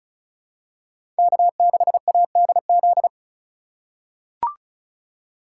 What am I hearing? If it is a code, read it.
Morse code, heard as a keyed tone, about 34 wpm, 700 Hz. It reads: K6ADZ
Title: USA Call Signs - 2x Character Spacing - 35wpm